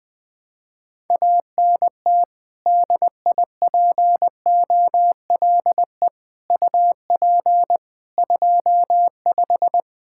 Morse code, heard as a keyed tone, about 20 wpm, 700 Hz. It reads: ANT DIPOLE UP 25
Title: Practice QSO - Exchange 2 - Info - 20wpm